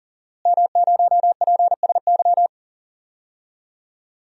Morse code, heard as a keyed tone, about 40 wpm, 690 Hz. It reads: M0PSY
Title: DXCC-100 Call Signs - 5 Characters - Repeated 2x - 40wpm